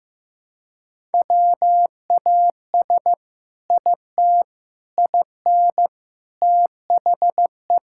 Morse code, heard as a keyed tone, about 15 wpm, 700 Hz. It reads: WAS IT IN THE
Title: Sentences from Top 1000 Words - Review Part 5 - 15wpm